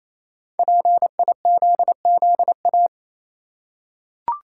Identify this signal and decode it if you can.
Morse code, heard as a keyed tone, about 28 wpm, 700 Hz. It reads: PIZZA
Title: Five Letter Words - 28wpm